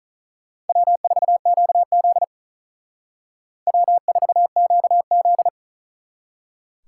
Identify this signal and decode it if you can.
Morse code, heard as a keyed tone, about 42 wpm, 700 Hz. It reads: W4QZ W4QZ
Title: Speed Racing - USA Call Signs 35wpm